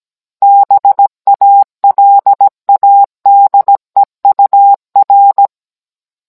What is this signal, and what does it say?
Morse code, heard as a keyed tone, about 17 wpm, 800 Hz. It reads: BALADEUR